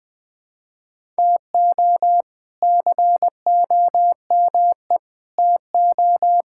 Morse code, heard as a keyed tone, about 20 wpm, 700 Hz. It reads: TO COME TO
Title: Sentences with 6-Word Phrases - 20wpm